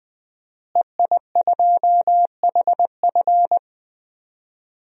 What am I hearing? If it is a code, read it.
Morse code, heard as a keyed tone, about 21 wpm, 700 Hz. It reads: EI2HF
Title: DXCC-100 Call Signs - 5 Characters - Repeated 2x - 20wpm